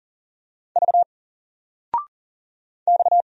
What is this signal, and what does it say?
Morse code, heard as a keyed tone, about 40 wpm, 700 Hz. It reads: V X
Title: Single Letter-Number 40wpm